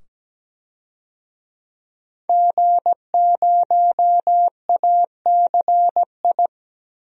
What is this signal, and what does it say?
Morse code, heard as a keyed tone, about 17 wpm, 700 Hz. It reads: G0ACI
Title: DXCC-100 Call Signs - 5 Characters - 17wpm